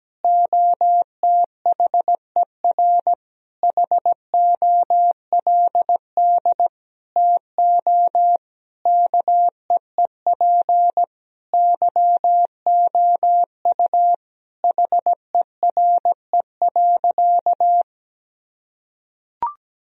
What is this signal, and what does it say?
Morse code, heard as a keyed tone, about 17 wpm, 700 Hz. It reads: OTHER HOLD TO KEEP YOU HERE.
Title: Sentences from Top 1000 Words - Review Part 2 - 17wpm